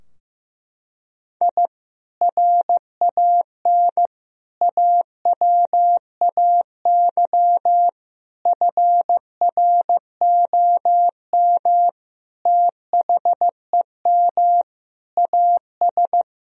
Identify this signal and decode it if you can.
Morse code, heard as a keyed tone, about 15 wpm, 700 Hz. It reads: I RAN AWAY FROM THEM AS
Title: Sentences from Top 1000 Words - Review Part 3 - 15wpm